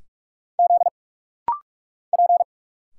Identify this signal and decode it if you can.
Morse code, heard as a keyed tone, about 46 wpm, 700 Hz. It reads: Z P
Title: Single Letters - Rapid-Fire 45wpm